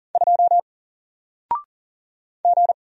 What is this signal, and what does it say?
Morse code, heard as a keyed tone, about 40 wpm, 700 Hz. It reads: 2 G
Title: Single Letter-Number 40wpm